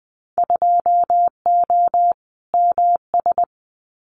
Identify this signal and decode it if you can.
Morse code, heard as a keyed tone, about 20 wpm, 700 Hz. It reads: 2O MS
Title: Winter Field Day Exchanges 20wpm